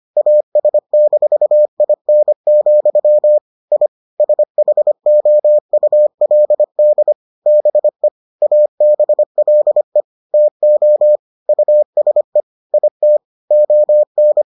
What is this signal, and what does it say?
Morse code, heard as a keyed tone, about 25 wpm, 600 Hz. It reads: AS-IN, I SHOULD BE ABLE TO USE IT ON